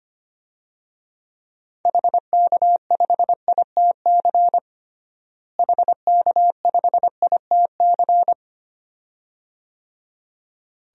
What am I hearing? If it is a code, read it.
Morse code, heard as a keyed tone, about 25 wpm, 700 Hz. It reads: HK5ITC HK5ITC
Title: DXCC-100 Call Signs - 6 Characters - Repeated 2x - 25wpm